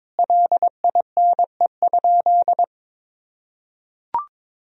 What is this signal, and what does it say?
Morse code, heard as a keyed tone, about 22 wpm, 700 Hz. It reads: LINE?
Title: Sentences from Top 1000 Words - Review Part 1 - 22wpm